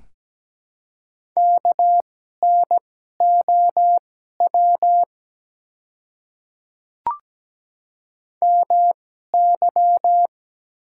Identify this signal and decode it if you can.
Morse code, heard as a keyed tone, about 17 wpm, spaced as for 12 wpm, 700 Hz. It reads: KNOW MY
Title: 2x Character Spacing - Top 100 Words - 17wpm